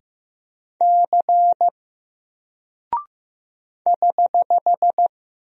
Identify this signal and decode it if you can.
Morse code, heard as a keyed tone, about 15 wpm, 700 Hz. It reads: C <HH>
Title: Single Character (Letter:Number:Punctuation:Prosign) 15wpm